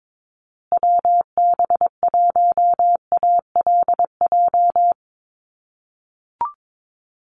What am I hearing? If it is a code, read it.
Morse code, heard as a keyed tone, about 22 wpm, 700 Hz. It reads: WB1ALJ
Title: USA Call Signs Repeated 2x 22wpm